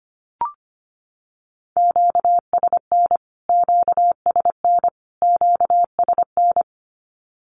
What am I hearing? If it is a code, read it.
Morse code, heard as a keyed tone, about 25 wpm, 700 Hz. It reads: QSN QSN QSN
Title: Q-Codes Repeated 3x 25wpm